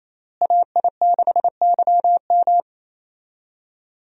Morse code, heard as a keyed tone, about 28 wpm, 700 Hz. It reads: AI6YM
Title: USA Call Signs - 5 Characters - Repeated 3x - 28wpm